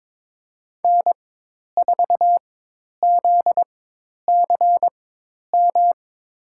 Morse code, heard as a keyed tone, about 22 wpm, spaced as for 10 wpm, 700 Hz. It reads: N4ZCM
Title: USA Call Signs - 4x Character Spacing - 22wpm